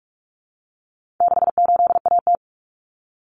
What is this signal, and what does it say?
Morse code, heard as a keyed tone, about 44 wpm, 700 Hz. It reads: 68AT
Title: Common Cut Numbers - 4-Digit 45wpm